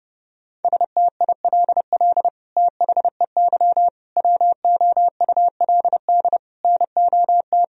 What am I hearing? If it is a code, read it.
Morse code, heard as a keyed tone, about 30 wpm, 700 Hz. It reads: STILL THEY WOULD NOT
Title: Sentences from Top 1000 Words - Review Part 1 - 30wpm